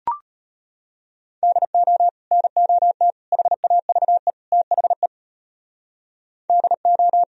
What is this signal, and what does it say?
Morse code, heard as a keyed tone, about 32 wpm, 700 Hz. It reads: DO NOT HAVE THE DO